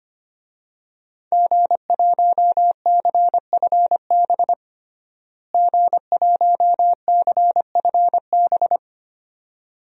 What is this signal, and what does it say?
Morse code, heard as a keyed tone, about 25 wpm, 700 Hz. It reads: G1CFB G1CFB
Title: DXCC-100 Call Signs - 5 Characters - Repeated 2x - 25wpm